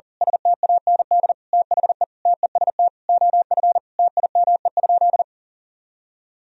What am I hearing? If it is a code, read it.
Morse code, heard as a keyed tone, about 40 wpm, 700 Hz. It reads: STAND THE TEST OF TIME?
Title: Sentences from Top 1000 Words - Review Part 3 - 40wpm